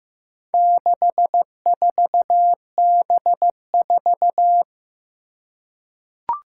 Morse code, heard as a keyed tone, about 16 wpm, 700 Hz. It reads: 64B4
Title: Four Challenging Letter-Number Combinations 15wpm